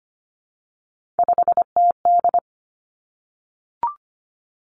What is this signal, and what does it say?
Morse code, heard as a keyed tone, about 25 wpm, 700 Hz. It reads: 5TD
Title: Three Challenging Letter-Number Combinations 25wpm